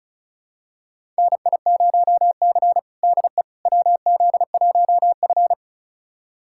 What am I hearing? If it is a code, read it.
Morse code, heard as a keyed tone, about 34 wpm, 700 Hz. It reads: NI0C DE WZ1F
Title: USA Callsign DE Callsign - 4 Characters - 35wpm